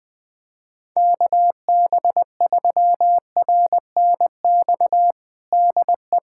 Morse code, heard as a keyed tone, about 20 wpm, 700 Hz. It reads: KB3RNX DE
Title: USA Callsign DE Callsign 20wpm